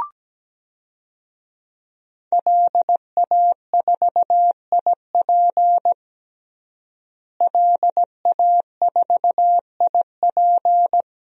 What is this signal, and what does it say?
Morse code, heard as a keyed tone, about 17 wpm, 700 Hz. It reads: LA4IP LA4IP
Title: DXCC-100 Call Signs - 5 Characters - Repeated 2x - 17wpm